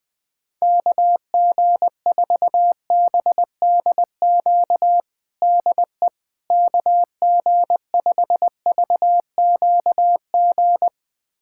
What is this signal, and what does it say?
Morse code, heard as a keyed tone, about 19 wpm, 700 Hz. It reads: KG4BDQ DE KG5VQG
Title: USA Callsign DE Callsign 20wpm